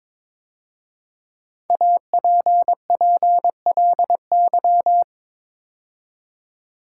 Morse code, heard as a keyed tone, about 22 wpm, 700 Hz. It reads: APPLY
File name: 5 Letter Words - First 3 Letters Given - 22wpm